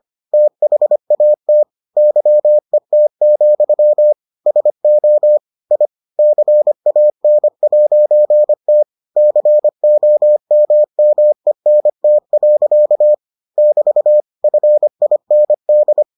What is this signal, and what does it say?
Morse code, heard as a keyed tone, about 25 wpm, 600 Hz. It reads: THAT YET, SO I CAN'T COMMENT. <BT> FIND